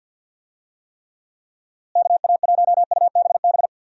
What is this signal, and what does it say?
Morse code, heard as a keyed tone, about 52 wpm, 700 Hz. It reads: KA1UBB
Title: USA Call Signs - 6 Characters - Repeated 2x - 50wpm